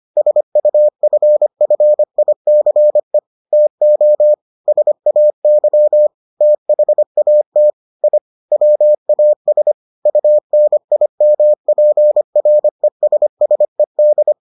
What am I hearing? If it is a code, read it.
Morse code, heard as a keyed tone, about 25 wpm, 600 Hz. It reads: SUFFICE TO SAY THAT I WAS UNIMPRESSED